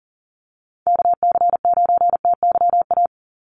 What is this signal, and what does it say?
Morse code, heard as a keyed tone, about 40 wpm, 700 Hz. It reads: KC9TYA
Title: NA SOTA Chaser Callsigns 40wpm